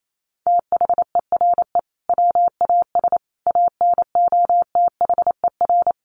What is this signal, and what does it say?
Morse code, heard as a keyed tone, about 28 wpm, 700 Hz. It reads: THERE WAS ANOTHER
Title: Sentences from Top 1300 Words 28wpm